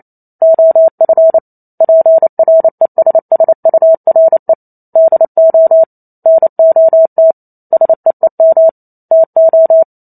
Morse code, heard as a keyed tone, about 28 wpm, 650 Hz. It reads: OF PRESSURE DO NOT SEEM TO